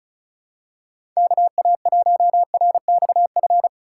Morse code, heard as a keyed tone, about 36 wpm, 700 Hz. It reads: KA1RXF
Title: USA Call Signs - 6 Characters - 3x Character Spacing - 35wpm